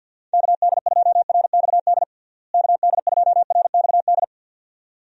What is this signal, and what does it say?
Morse code, heard as a keyed tone, about 52 wpm, 700 Hz. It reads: KD2RXD KD2RXD
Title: USA Call Signs - 6 Characters - Repeated 2x - 50wpm